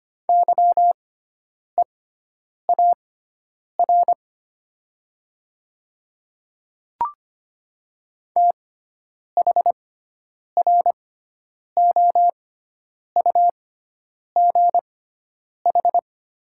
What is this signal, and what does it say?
Morse code, heard as a keyed tone, about 25 wpm, spaced as for 9 wpm, 700 Hz. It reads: YEAR THROUGH
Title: 6x Character Spacing - Top 300 Words - 25wpm